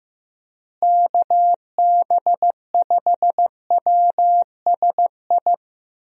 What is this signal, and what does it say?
Morse code, heard as a keyed tone, about 15 wpm, 700 Hz. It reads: KB5WSI
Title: USA Call Signs - 6 Characters - Repeated 3x - 15wpm